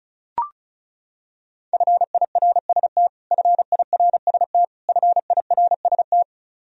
Morse code, heard as a keyed tone, about 34 wpm, 700 Hz. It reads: FIRST FIRST FIRST
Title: Top 100 Words Repeated 35wpm